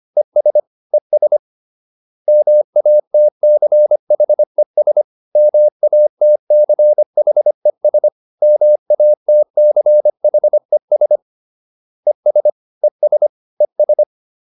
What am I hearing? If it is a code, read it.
Morse code, heard as a keyed tone, about 25 wpm, 600 Hz. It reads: ES ES MATCHES MATCHES MATCHES ES ES ES